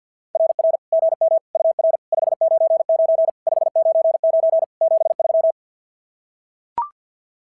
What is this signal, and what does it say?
Morse code, heard as a keyed tone, about 46 wpm, 660 Hz. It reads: RR GM UR 599 599 73
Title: SOTA QSO Practice 50wpm